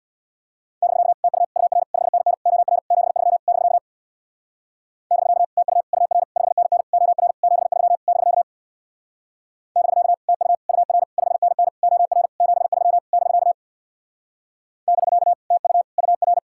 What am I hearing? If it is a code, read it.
Morse code, heard as a keyed tone, about 64 wpm, 700 Hz. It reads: <BK> TU UR 5NN OR 73 <BK> <BK> TU UR 5NN OR 73 <BK> <BK> TU UR 5NN OR 73 <BK> <BK> TU UR